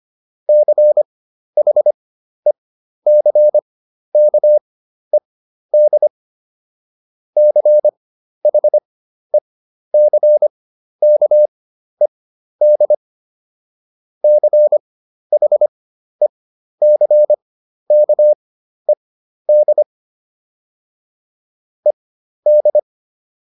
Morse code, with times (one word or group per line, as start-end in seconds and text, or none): 0.5-6.1 CHECKED
7.4-12.9 CHECKED
14.2-19.8 CHECKED
21.9-22.8 ED